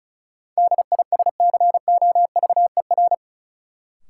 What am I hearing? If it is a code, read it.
Morse code, heard as a keyed tone, about 36 wpm, 700 Hz. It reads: DISCOVER
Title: Sentences from Top 1100 Words - New Word Drill - 35wpm